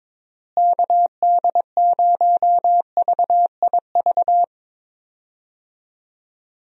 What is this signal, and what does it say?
Morse code, heard as a keyed tone, about 22 wpm, 700 Hz. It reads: KD0VIV
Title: USA Call Signs - 6 Characters - Repeated 3x - 22wpm